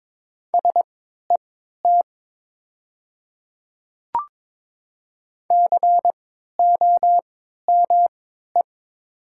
Morse code, heard as a keyed tone, about 22 wpm, spaced as for 12 wpm, 700 Hz. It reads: SET COME
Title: 3x Character Spacing - Top 300 Words - 22wpm